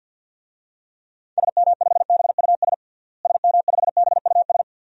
Morse code, heard as a keyed tone, about 49 wpm, 700 Hz. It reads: SM5BUS SM5BUS
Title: DXCC-100 Call Signs - 6 Characters - Repeated 2x - 50wpm